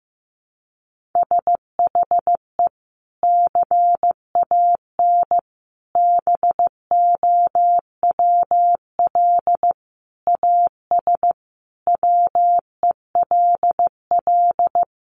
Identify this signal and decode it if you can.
Morse code, heard as a keyed tone, about 15 wpm, 700 Hz. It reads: SHE CAN BOWL AS WELL
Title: Sentences from Top 1500 Words 15wpm